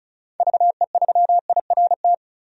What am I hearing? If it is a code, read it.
Morse code, heard as a keyed tone, about 34 wpm, 700 Hz. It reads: VE3IRT
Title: DXCC-100 Call Signs - 6 Characters - 35wpm